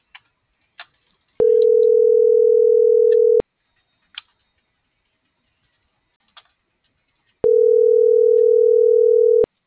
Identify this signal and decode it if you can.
Ringback tone (North America), 440 + 480 Hz.
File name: CENTRAL MORENA / out ruth.skyangel.24 - .4421